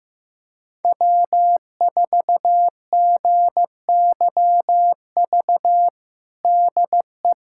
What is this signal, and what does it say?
Morse code, heard as a keyed tone, about 16 wpm, 700 Hz. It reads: W4GYV DE
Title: USA Callsign DE Callsign - 5 Characters - 15wpm